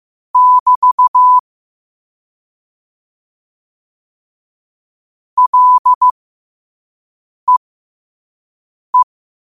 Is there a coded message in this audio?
Yes: Morse code from 0.4 to 9.0 s, heard as a keyed tone, about 15 wpm, spaced as for 5 wpm, 1000 Hz. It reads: <BT> LEE